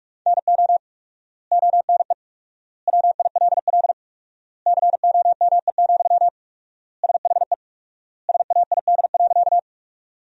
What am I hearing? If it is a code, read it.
Morse code, heard as a keyed tone, about 41 wpm, 700 Hz. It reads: NO ONE WILL COME, SHE SAID.
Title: 4x Word-Spacing - Sentences from Top 100 Words - 45wpm